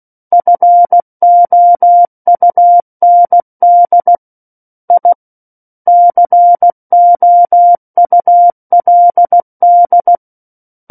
Morse code, heard as a keyed tone, about 16 wpm, 700 Hz. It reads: FOUND I COULD
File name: Book One / 07-Chapter 7 - 16 WPM